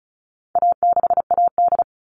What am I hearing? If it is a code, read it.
Morse code, heard as a keyed tone, about 34 wpm, 700 Hz. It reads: A6AD
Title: All Cut Numbers - 4-Digit 35wpm